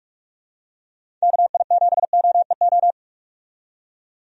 Morse code, heard as a keyed tone, about 46 wpm, 700 Hz. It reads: KI7OEO
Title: USA Call Signs - 6 Characters - 45wpm